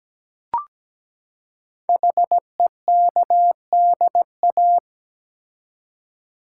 Morse code, heard as a keyed tone, about 17 wpm, 700 Hz. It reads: HEKDA
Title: Code Groups - 5 Letters - 17wpm